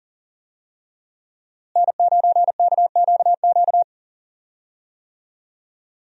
Morse code, heard as a keyed tone, about 40 wpm, 700 Hz. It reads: N9KQQ
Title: USA Call Signs - 5 Characters - Repeated 2x - 40wpm